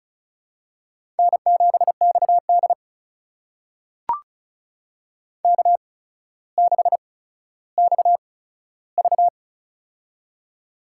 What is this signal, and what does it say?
Morse code, heard as a keyed tone, about 36 wpm, spaced as for 15 wpm, 700 Hz. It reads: N7XD K6XV
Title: USA Call Signs - 4 Characters - 8x Character Spacing - 35wpm